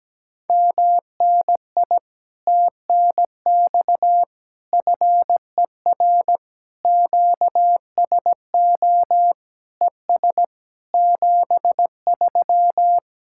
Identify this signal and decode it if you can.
Morse code, heard as a keyed tone, about 17 wpm, 700 Hz. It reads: MNI TNX FER QSO ES 73
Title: Speed-Racing Common QSO Phrases - 17wpm